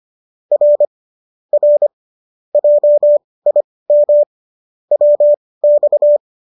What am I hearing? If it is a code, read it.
Morse code, heard as a keyed tone, about 25 wpm, spaced as for 18 wpm, 590 Hz. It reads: R R JIM WX
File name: qso203 18